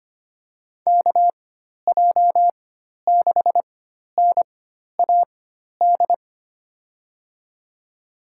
Morse code, heard as a keyed tone, about 25 wpm, spaced as for 12 wpm, 700 Hz. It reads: KJ6NAD
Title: USA Call Signs - 4x Character Spacing - 25wpm